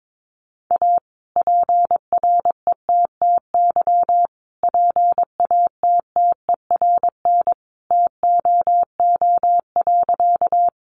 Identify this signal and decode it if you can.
Morse code, heard as a keyed tone, about 22 wpm, 700 Hz. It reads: A PRETTY PATTERN TOO.
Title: Sentences from Top 1000 Words 22wpm